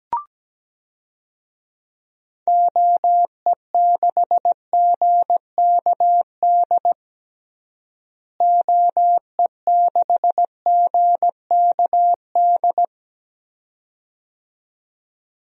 Morse code, heard as a keyed tone, about 17 wpm, 700 Hz. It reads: OE6GKD OE6GKD
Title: DXCC-100 Call Signs - 6 Characters - Repeated 2x - 17wpm